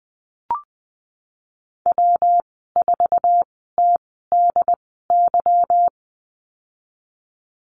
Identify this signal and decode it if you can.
Morse code, heard as a keyed tone, about 20 wpm, spaced as for 15 wpm, 700 Hz. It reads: W4TDY
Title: 186 Lesson 30g - Letter b - Call Signs 20wpm